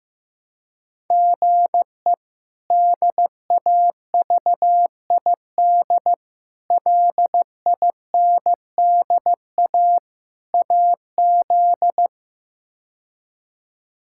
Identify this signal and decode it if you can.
Morse code, heard as a keyed tone, about 15 wpm, 700 Hz. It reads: GE DAVID LINDA AZ
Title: K1USN SST QSO Practice - 15wpm